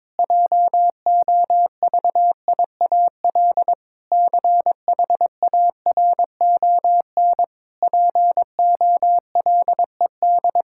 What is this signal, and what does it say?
Morse code, heard as a keyed tone, about 22 wpm, 700 Hz. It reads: JOVIAL CHARON POLED
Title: Pangram Challenge v2 - 22wpm